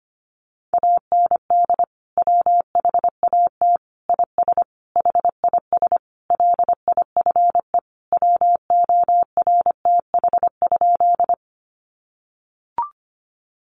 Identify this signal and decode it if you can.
Morse code, heard as a keyed tone, about 25 wpm, 700 Hz. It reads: AND WHAT IS HIS LIFE WORTH?
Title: Sentences from Top 800 Words 25wpm